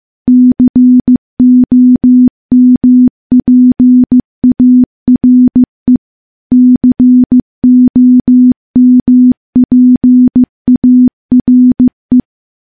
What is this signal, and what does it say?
Morse code, heard as a keyed tone, about 15 wpm, 250 Hz. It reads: COMPARE COMPARE
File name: morse-training-15WPM